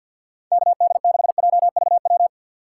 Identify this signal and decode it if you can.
Morse code, heard as a keyed tone, about 52 wpm, 690 Hz. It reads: KD6JVW
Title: USA Call Signs - 6 Characters - 3x Character Spacing - 50wpm